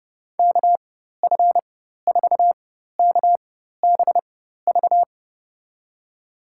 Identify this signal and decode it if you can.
Morse code, heard as a keyed tone, about 32 wpm, spaced as for 14 wpm, 700 Hz. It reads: KF4KBV
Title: USA Call Signs - 6 Characters - 4x Character Spacing - 30wpm